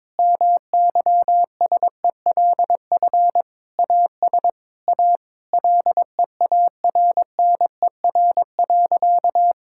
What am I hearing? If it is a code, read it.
Morse code, heard as a keyed tone, about 22 wpm, 700 Hz. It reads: MYSELF AS A LEARNER.
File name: practice / 22wpm / Sentences with 8-Word Phrases - 22wpm